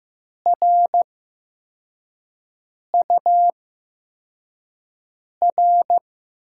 Morse code, heard as a keyed tone, about 15 wpm, spaced as for 4 wpm, 700 Hz. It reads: RUR
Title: USA Call Signs - 6 Characters - 8x Character Spacing - 15wpm